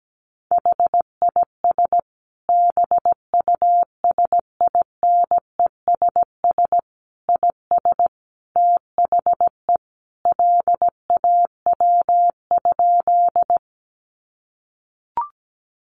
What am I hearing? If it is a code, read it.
Morse code, heard as a keyed tone, about 17 wpm, 700 Hz. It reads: HIS BUSINESS IS THE LAW?
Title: Sentences from Top 600 Words 17wpm